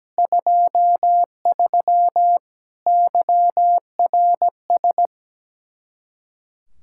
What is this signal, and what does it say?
Morse code, heard as a keyed tone, about 17 wpm, 700 Hz. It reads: 23 YRS
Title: Practice QSO - Exchange 3 - More Info - 17wpm